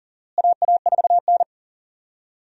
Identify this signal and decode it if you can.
Morse code, heard as a keyed tone, about 40 wpm, 700 Hz. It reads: AA4N
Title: USA Call Signs - 4 Characters - Repeated 3x - 40wpm